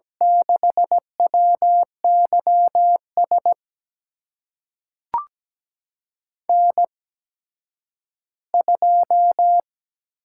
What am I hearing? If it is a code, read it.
Morse code, heard as a keyed tone, about 17 wpm, spaced as for 10 wpm, 700 Hz. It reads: 6WYS N2
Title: USA Call Signs - 8x Character Spacing - 17wpm